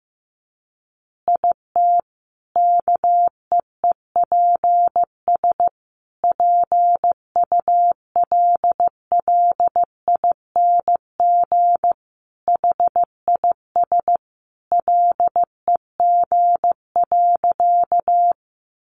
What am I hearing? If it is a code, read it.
Morse code, heard as a keyed tone, about 15 wpm, 700 Hz. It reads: IT KEEPS PULLING HIS LEG.
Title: Additional Easy Phrases 15wpm